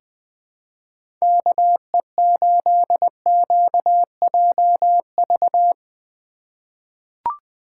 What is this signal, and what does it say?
Morse code, heard as a keyed tone, about 21 wpm, 700 Hz. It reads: KE8QJV
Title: 2023 POTA Activator Call Signs - 20wpm